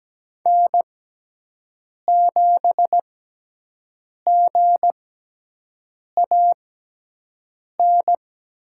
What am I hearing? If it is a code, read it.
Morse code, heard as a keyed tone, about 17 wpm, spaced as for 6 wpm, 700 Hz. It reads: N7GAN
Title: USA Call Signs - 6x Character Spacing - 17wpm